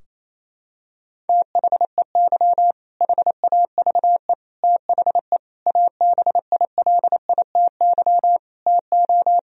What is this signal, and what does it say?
Morse code, heard as a keyed tone, about 27 wpm, 700 Hz. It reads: THEY HAVE THE ABILITY TO
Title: Sentences from Top 1000 Words - Review Part 3 - 28wpm